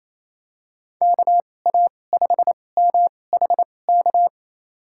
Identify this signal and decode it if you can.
Morse code, heard as a keyed tone, about 28 wpm, spaced as for 20 wpm, 700 Hz. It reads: KA5MHK
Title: USA Call Signs - 6 Characters - 2x Character Spacing - 28wpm